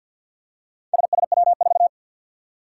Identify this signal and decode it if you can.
Morse code, heard as a keyed tone, about 49 wpm, 700 Hz. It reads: SSW4
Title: Four Challenging Letter-Number Combinations 50wpm